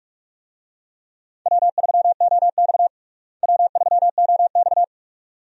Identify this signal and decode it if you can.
Morse code, heard as a keyed tone, about 46 wpm, 700 Hz. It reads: W3OX W3OX
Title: USA Call Signs - 4 Characters - Repeated 2x - 45wpm